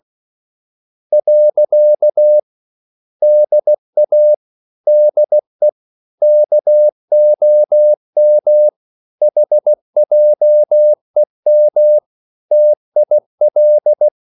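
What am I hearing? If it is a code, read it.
Morse code, heard as a keyed tone, about 16 wpm, 600 Hz. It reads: . DA DE KOM HJEM TIL
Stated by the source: Slow (QRS) Morse code at 16 wpm, from CW med Gnister.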